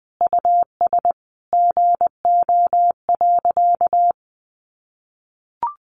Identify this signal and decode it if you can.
Morse code, heard as a keyed tone, about 20 wpm, 700 Hz. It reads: US GO.
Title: Sentences from Top 400 Words 20wpm